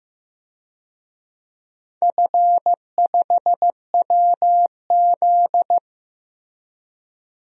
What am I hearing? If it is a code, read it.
Morse code, heard as a keyed tone, about 15 wpm, 700 Hz. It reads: F5WZ
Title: DXCC-100 Call Signs - 4 Characters - Repeated 3x - 15wpm